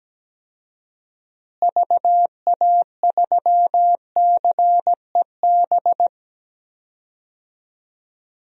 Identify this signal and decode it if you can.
Morse code, heard as a keyed tone, about 17 wpm, 700 Hz. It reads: VA3CEB
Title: DXCC-100 Call Signs - 6 Characters - Repeated 2x - 17wpm